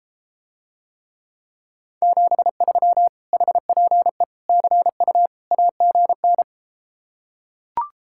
Morse code, heard as a keyed tone, about 33 wpm, 700 Hz. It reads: 73 HPE CU AGN